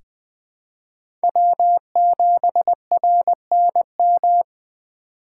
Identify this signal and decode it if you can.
Morse code, heard as a keyed tone, about 20 wpm, 700 Hz. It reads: W7RNM
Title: USA Call Signs - 5 Characters - 4x Character Spacing - 20wpm